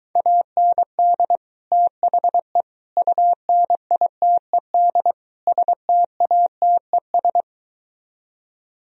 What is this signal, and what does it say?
Morse code, heard as a keyed tone, about 23 wpm, 700 Hz. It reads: AND THE UNITED STATES